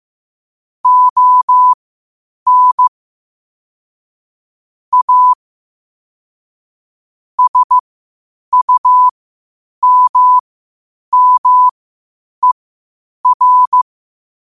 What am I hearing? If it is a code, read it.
Morse code, heard as a keyed tone, about 15 wpm, spaced as for 8 wpm, 1000 Hz. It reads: ON A SUMMER
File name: qotdfiles / 8wpm